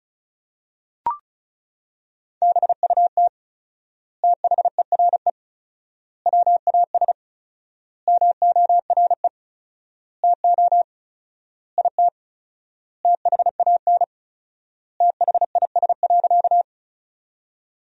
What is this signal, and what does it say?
Morse code, heard as a keyed tone, about 36 wpm, 700 Hz. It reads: BUT THERE WAS MORE TO IT THAN THIS.
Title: 4x Word-Spacing - Sentences from Top 100 Words - 35wpm